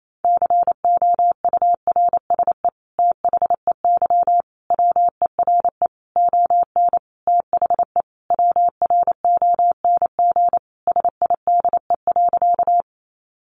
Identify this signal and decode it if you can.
Morse code, heard as a keyed tone, about 28 wpm, 700 Hz. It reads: COURSE THEY WERE ON THE WRONG SIDE.
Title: Sentences from Top 900 Words 28wpm